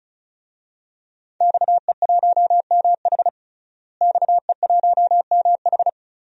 Morse code, heard as a keyed tone, about 34 wpm, 700 Hz. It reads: XE1MH XE1MH
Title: DXCC-100 Call Signs - 5 Characters - Repeated 3x - 35wpm